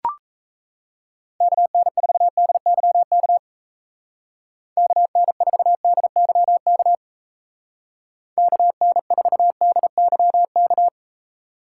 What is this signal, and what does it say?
Morse code, heard as a keyed tone, about 36 wpm, 700 Hz. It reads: KN4DYK KN4DYK KN4DYK